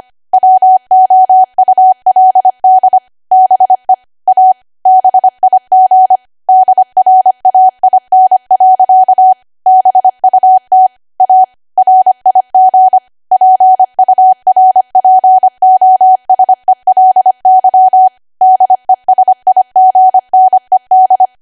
Morse code, heard as a keyed tone, about 25 wpm, 750 Hz. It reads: WOULD BE A BIG DRAIN. BUT A RIG PURPOSELY DESIGNED
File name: Archive / 25 WPM / 25WPM